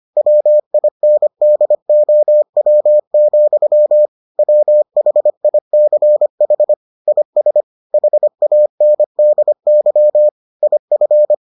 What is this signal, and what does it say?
Morse code, heard as a keyed tone, about 25 wpm, 600 Hz. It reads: WINDOW, WHICH IS HANDY IF